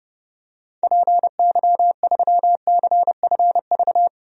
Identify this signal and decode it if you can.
Morse code, heard as a keyed tone, about 32 wpm, 700 Hz. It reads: PY3CFV
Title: DXCC-100 Call Signs - 6 Characters - Repeated 2x - 30wpm